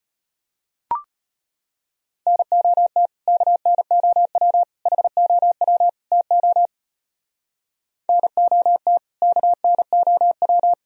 Morse code, heard as a keyed tone, about 38 wpm, 700 Hz. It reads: NOT KNOW HOW TO NOT KNOW